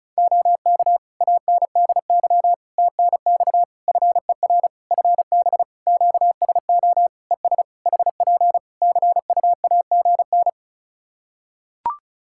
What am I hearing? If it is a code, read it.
Morse code, heard as a keyed tone, about 36 wpm, 700 Hz. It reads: OK ANDY TNX FER FB QSO ES HP CUAGN
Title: K1USN SST QSO Practice 35wpm